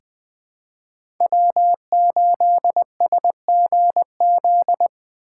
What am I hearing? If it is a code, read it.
Morse code, heard as a keyed tone, about 20 wpm, 700 Hz. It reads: W8SGZ
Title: USA Call Signs - 5 Characters - 3x Character Spacing - 20wpm